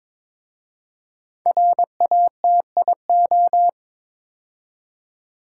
Morse code, heard as a keyed tone, about 22 wpm, 700 Hz. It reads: RATIO
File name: 5 Letter Words - First 3 Letters Given - 22wpm